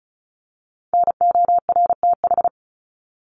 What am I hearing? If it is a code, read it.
Morse code, heard as a keyed tone, about 34 wpm, 700 Hz. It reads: NORTH
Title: Top 300 Words Repeated 35wpm